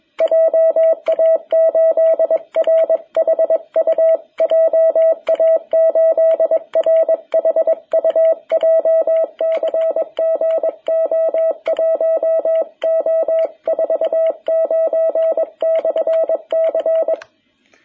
Morse code, heard as a keyed tone, about 21 wpm, 660 Hz. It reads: JA8RHU JA8RHU JCG O1O49/C
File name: jh8cbh / 2021cw